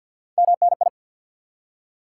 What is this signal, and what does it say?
Morse code, heard as a keyed tone, about 52 wpm, 700 Hz. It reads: MNI
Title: Top 90 QSO Elements 50wpm